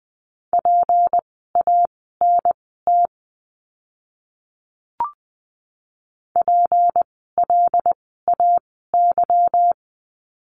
Morse code, heard as a keyed tone, about 20 wpm, spaced as for 15 wpm, 700 Hz. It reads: PANT PLAY
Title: 153 Lesson 25f - Letter p - Words 20wpm